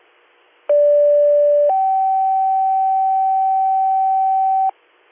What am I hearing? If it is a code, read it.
Two-tone page, heard as a one-second tone, then a three-second tone, about 580 then 770 Hz, with noise. It is heard from 0.7 to 4.7 s.